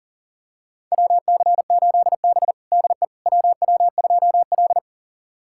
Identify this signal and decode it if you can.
Morse code, heard as a keyed tone, about 40 wpm, 700 Hz. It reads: WC8B DE WW2L
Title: USA Callsign DE Callsign - 4 Characters - 40wpm